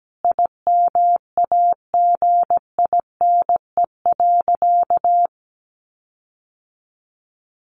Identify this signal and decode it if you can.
Morse code, heard as a keyed tone, about 17 wpm, 700 Hz. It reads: IMAGINE.
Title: Sentences from Top 1500 Words 17wpm